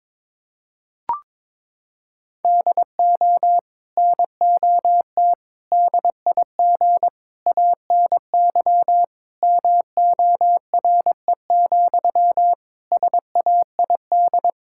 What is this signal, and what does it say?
Morse code, heard as a keyed tone, about 22 wpm, 700 Hz. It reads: DO NOT DIG ANY MORE, SAID
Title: Sentences from Top 1600 Words 22wpm